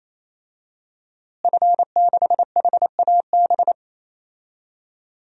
Morse code, heard as a keyed tone, about 27 wpm, 700 Hz. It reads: F6HAB
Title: DXCC-100 Call Signs - 5 Characters - Repeated 3x - 28wpm